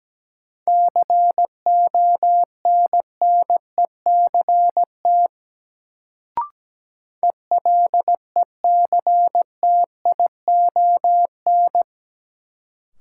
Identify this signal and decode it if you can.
Morse code, heard as a keyed tone, about 17 wpm, 700 Hz. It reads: CONNECT ELECTION
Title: Sentences from Top 1500 Words - New Word Drill - 17wpm